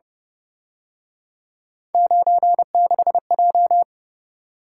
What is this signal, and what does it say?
Morse code, heard as a keyed tone, about 29 wpm, 700 Hz. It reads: 96J
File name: Speed-Racing - Three Letter-Number Combinations - 25wpm